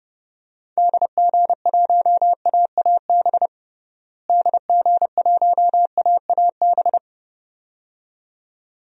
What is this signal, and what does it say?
Morse code, heard as a keyed tone, about 32 wpm, 700 Hz. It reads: DG1AAB DG1AAB
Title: DXCC-100 Call Signs - 6 Characters - Repeated 2x - 30wpm